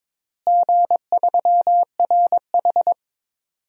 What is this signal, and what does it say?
Morse code, heard as a keyed tone, about 22 wpm, 700 Hz. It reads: G3RH
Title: DXCC-100 Call Signs - 4 Characters - Repeated 3x - 22wpm